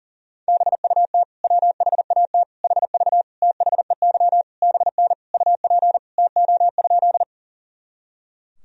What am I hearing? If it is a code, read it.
Morse code, heard as a keyed tone, about 40 wpm, 700 Hz. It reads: BUT WHAT HV THEY BN UP TO?
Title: Sentences from Top 100 Words and Abbreviations 40wpm